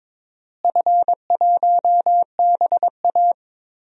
Morse code, heard as a keyed tone, about 22 wpm, 700 Hz. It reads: F1BA
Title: DXCC-100 Call Signs - 4 Characters - Repeated 2x - 22wpm